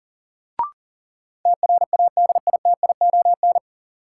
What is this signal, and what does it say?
Morse code, heard as a keyed tone, about 40 wpm, 700 Hz. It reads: TRADITION
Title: Sentences from Top 1500 Words - New Word Drill Alt - 40wpm